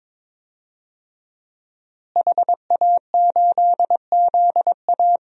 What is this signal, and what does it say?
Morse code, heard as a keyed tone, about 23 wpm, 700 Hz. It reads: HA8ZA
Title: DXCC-100 Call Signs - 5 Characters - Repeated 3x - 22wpm